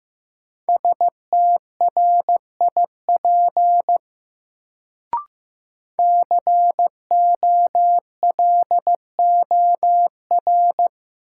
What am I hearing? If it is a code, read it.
Morse code, heard as a keyed tone, about 15 wpm, 700 Hz. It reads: STRIP COLOR
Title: Sentences from Top 1600 Words - New Word Drill Alt - 15wpm